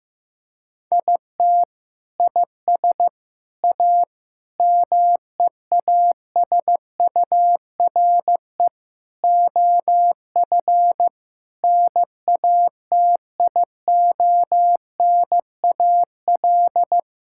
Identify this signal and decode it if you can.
Morse code, heard as a keyed tone, about 15 wpm, 700 Hz. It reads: IT IS A MEASURE OF NATIONAL
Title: Sentences from Top 1600 Words 15wpm